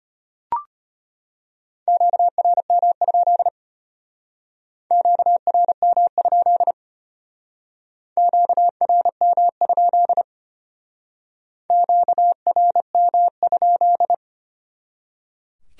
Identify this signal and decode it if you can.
Morse code, heard as a keyed tone, about 33 wpm, 700 Hz. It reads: QRM? QRM? QRM? QRM?